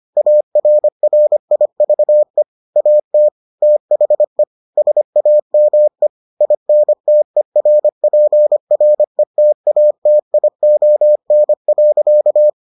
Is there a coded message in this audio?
Yes: Morse code from 0.2 to 12.5 s, heard as a keyed tone, about 25 wpm, 600 Hz. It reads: ARRIVE AT THE SAME INTERPRETATION.